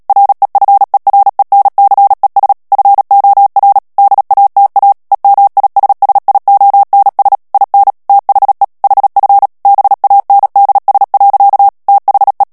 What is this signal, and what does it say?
Morse code, heard as a keyed tone, about 37 wpm, 800 Hz. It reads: REFERENCES FOR DATA EMISSIONS IN THE HF BANDS. THE
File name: morse mp3 / ANGLAIS texte clair 3 vitesse 35 mots minute